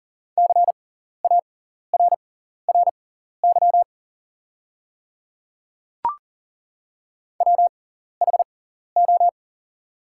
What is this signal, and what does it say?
Morse code, heard as a keyed tone, about 40 wpm, spaced as for 14 wpm, 690 Hz. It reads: CARRY WHO
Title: 6x Character Spacing - Top 300 Words - 40wpm